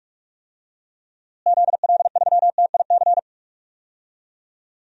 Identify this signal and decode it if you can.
Morse code, heard as a keyed tone, about 46 wpm, 700 Hz. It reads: ZL3TIC
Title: DXCC-100 Call Signs - 6 Characters - Repeated 2x - 45wpm